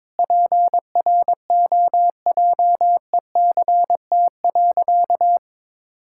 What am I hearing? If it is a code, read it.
Morse code, heard as a keyed tone, about 22 wpm, 700 Hz. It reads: PROJECT.
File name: practice / 22wpm / Sentences with 4-Word Phrases - 22wpm